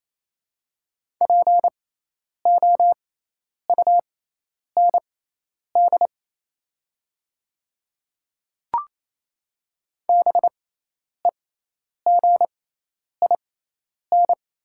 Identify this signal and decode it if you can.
Morse code, heard as a keyed tone, about 28 wpm, spaced as for 10 wpm, 690 Hz. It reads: POUND BEGIN
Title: 6x Character Spacing - Top 500 Words - 28wpm